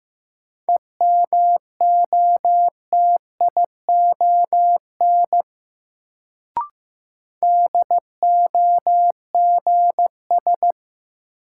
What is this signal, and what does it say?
Morse code, heard as a keyed tone, about 15 wpm, 700 Hz. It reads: EMOTION DOGS
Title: Sentences from Top 1600 Words - New Word Drill - 15wpm